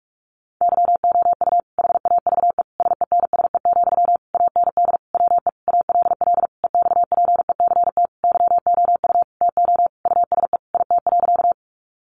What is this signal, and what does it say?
Morse code, heard as a keyed tone, about 44 wpm, 700 Hz. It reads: YOU HAVE SENSE, AND WE ALL EXPECT YOU TO USE IT.
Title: Sentences from Top 1100 Words 45wpm